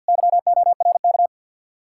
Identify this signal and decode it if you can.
Morse code, heard as a keyed tone, about 61 wpm, 700 Hz. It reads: YORK
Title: Speed-Racing Common QSO Phrases - 50wpm